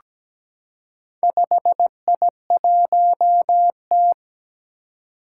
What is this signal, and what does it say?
Morse code, heard as a keyed tone, about 17 wpm, 700 Hz. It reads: 5I1T
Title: Four Challenging Letter-Number Combinations 17wpm